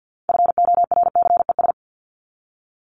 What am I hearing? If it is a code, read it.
Morse code, heard as a keyed tone, about 48 wpm, 700 Hz. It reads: FORCES
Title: ES:S Suffix - Top 100 Words with Base Words 50wpm